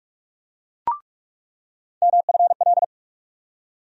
Morse code, heard as a keyed tone, about 43 wpm, 700 Hz. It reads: MFL
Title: USA Call Sign Postfixes - 3 Letters 45wpm